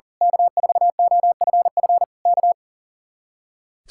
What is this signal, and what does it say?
Morse code, heard as a keyed tone, about 38 wpm, 700 Hz. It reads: K4OFF K
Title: Speed-Racing Common QSO Phrases - 40wpm